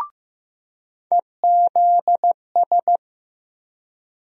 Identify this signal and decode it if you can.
Morse code, heard as a keyed tone, about 15 wpm, 700 Hz. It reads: EZS
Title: USA Call Sign Postfixes - 3 Letters 15wpm